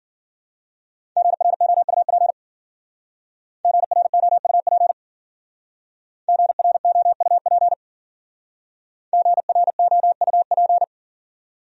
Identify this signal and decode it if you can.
Morse code, heard as a keyed tone, about 51 wpm, 700 Hz. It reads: GROUP GROUP GROUP GROUP